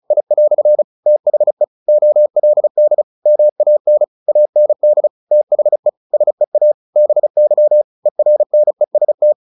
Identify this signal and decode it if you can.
Morse code, heard as a keyed tone, about 34 wpm, 600 Hz. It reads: I" THE OLD MAN AND THE SEA BY ERNEST
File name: the old man and the sea 2 35wpm